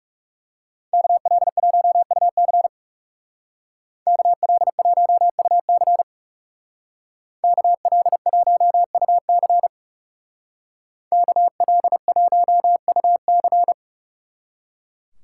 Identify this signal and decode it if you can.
Morse code, heard as a keyed tone, about 37 wpm, 700 Hz. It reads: KL1UC KL1UC KL1UC KL1UC
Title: Speed Racing - USA 2x2 Call Signs 30wpm